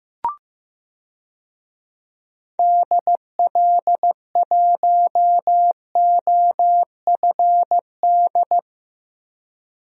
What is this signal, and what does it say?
Morse code, heard as a keyed tone, about 15 wpm, 700 Hz. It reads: DL1OFD
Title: DXCC-100 Call Signs - 6 Characters - Repeated 3x - 15wpm